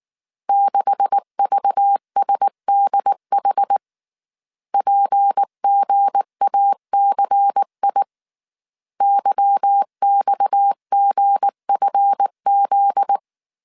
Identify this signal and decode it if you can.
Morse code, heard as a keyed tone, about 19 wpm, 800 Hz. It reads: 6VSDH PGACI YXGFZ
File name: CW / DET / 1 9 1500C S A 18wpm 2459